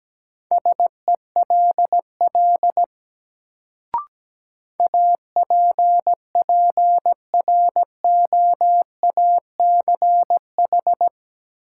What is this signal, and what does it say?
Morse code, heard as a keyed tone, about 17 wpm, 700 Hz. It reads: SELL APPROACH
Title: Sentences from Top 1000 Words - Review Part 6 - New Word Drill - 17wpm